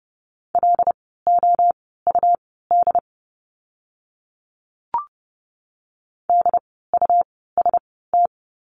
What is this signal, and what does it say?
Morse code, heard as a keyed tone, about 30 wpm, spaced as for 17 wpm, 700 Hz. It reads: LOUD DUST
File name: Lesson 17f - Letter u - Words 30wpm